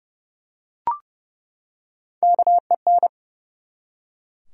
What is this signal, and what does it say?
Morse code, heard as a keyed tone, about 30 wpm, 700 Hz. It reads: KEN
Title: USA Common Names - Max 5 Letters - 30wpm